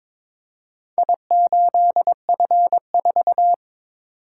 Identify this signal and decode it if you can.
Morse code, heard as a keyed tone, about 23 wpm, 700 Hz. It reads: I8F4
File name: practice / 22wpm / Four Letter-Number Combinations - 22wpm